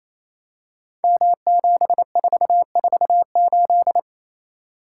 Morse code, heard as a keyed tone, about 28 wpm, 700 Hz. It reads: M7448
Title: Code Groups - 5 Letter:Numbers - 28wpm